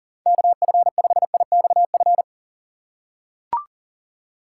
Morse code, heard as a keyed tone, about 40 wpm, 700 Hz. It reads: KF5IXF
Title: USA Call Signs - 6 Characters - 6x Character Spacing - 40wpm